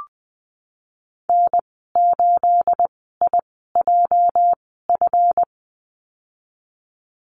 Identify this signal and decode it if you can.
Morse code, heard as a keyed tone, about 20 wpm, spaced as for 15 wpm, 700 Hz. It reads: N8IJF
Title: 231 Lesson 37g - Letter x - Call Signs 20wpm